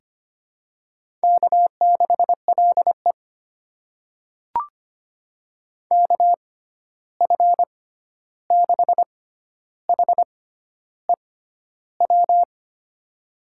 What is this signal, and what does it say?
Morse code, heard as a keyed tone, about 25 wpm, spaced as for 11 wpm, 700 Hz. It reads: K6LE KF6HEW
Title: USA Call Signs - 6x Character Spacing - 25wpm